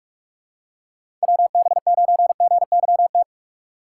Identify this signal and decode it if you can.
Morse code, heard as a keyed tone, about 46 wpm, 700 Hz. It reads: WB9GYT
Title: USA Call Signs - 4x Character Spacing - 45wpm